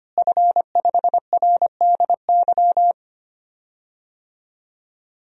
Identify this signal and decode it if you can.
Morse code, heard as a keyed tone, about 25 wpm, 700 Hz. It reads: F5RDY
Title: DXCC-100 Call Signs - 5 Characters - Repeated 3x - 25wpm